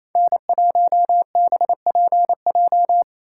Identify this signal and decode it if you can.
Morse code, heard as a keyed tone, about 27 wpm, 690 Hz. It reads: N1BPJ
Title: USA Call Signs - 3x Character Spacing - 28wpm